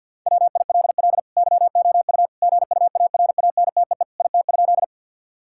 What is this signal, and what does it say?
Morse code, heard as a keyed tone, about 50 wpm, 700 Hz. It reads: WILL YOU GUARANTEE IT?
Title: Sentences from Top 1600 Words 50wpm